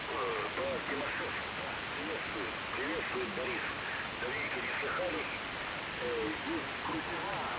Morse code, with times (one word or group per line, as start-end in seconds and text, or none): none